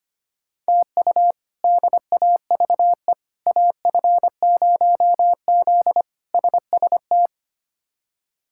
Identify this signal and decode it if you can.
Morse code, heard as a keyed tone, about 25 wpm, 700 Hz. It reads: TU DAVE AF0Z SST
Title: K1USN SST QSO Practice - 25wpm